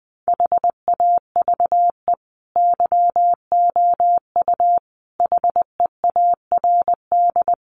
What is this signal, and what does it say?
Morse code, heard as a keyed tone, about 20 wpm, 700 Hz. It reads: HAVE YOU HEARD
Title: Sentences from Top 1000 Words 20wpm